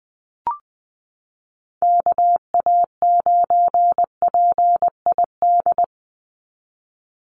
Morse code, heard as a keyed tone, about 20 wpm, 700 Hz. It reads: KA9PID
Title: USA Call Signs 20wpm